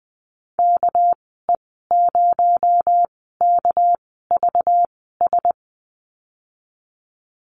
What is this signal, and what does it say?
Morse code, heard as a keyed tone, about 20 wpm, spaced as for 14 wpm, 700 Hz. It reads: KE0KVS
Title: 219 Lesson 35g - Number 0 - Call Signs 20wpm